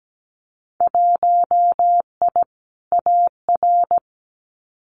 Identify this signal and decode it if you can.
Morse code, heard as a keyed tone, about 17 wpm, 700 Hz. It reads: 1I AR
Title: Winter Field Day Exchanges Encore 17wpm